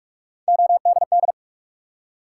Morse code, heard as a keyed tone, about 44 wpm, 700 Hz. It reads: ODD
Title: Three Letter Words - 45wpm